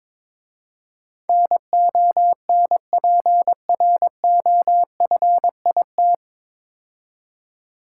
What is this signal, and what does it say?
Morse code, heard as a keyed tone, about 21 wpm, 700 Hz. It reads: NONPROFIT
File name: 9 Letter Words - First 3 Letters Given - 22wpm